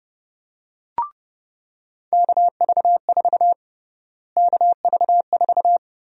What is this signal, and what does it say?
Morse code, heard as a keyed tone, about 32 wpm, 700 Hz. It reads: KV4 KV4
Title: USA Call Sign Prefixes - Repeated 2x - 30wpm